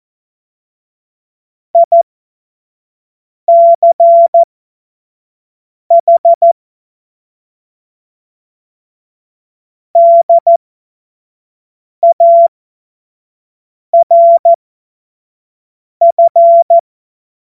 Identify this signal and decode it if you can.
Morse code, heard as a keyed tone, about 14 wpm, spaced as for 5 wpm, 680 Hz. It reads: ICH DARF
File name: Morsekurs / Witze / Witz 05 März 2024 25 F70